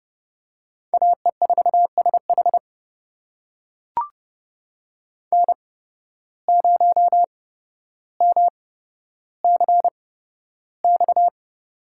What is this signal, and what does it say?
Morse code, heard as a keyed tone, about 31 wpm, spaced as for 13 wpm, 700 Hz. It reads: AE4SH N0MCX
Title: USA Call Signs - 5 Characters - 8x Character Spacing - 30wpm